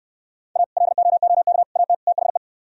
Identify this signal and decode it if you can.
Morse code, heard as a keyed tone, about 65 wpm, 700 Hz. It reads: A LOOK AT THE